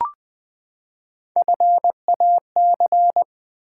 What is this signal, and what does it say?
Morse code, heard as a keyed tone, about 20 wpm, 700 Hz. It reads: FAC
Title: Build A Word - 3 Letters At A Time - 6-Letter Words - 20wpm